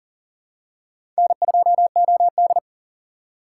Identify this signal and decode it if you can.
Morse code, heard as a keyed tone, about 40 wpm, 690 Hz. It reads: N2OD
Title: USA Call Signs - 4 Characters - 6x Character Spacing - 40wpm